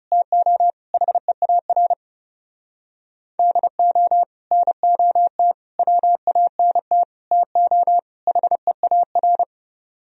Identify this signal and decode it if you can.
Morse code, heard as a keyed tone, about 31 wpm, 700 Hz. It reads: TO HEAR DO NOT WANT TO HEAR
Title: Speed-Racing - Sets of 5 Words 30wpm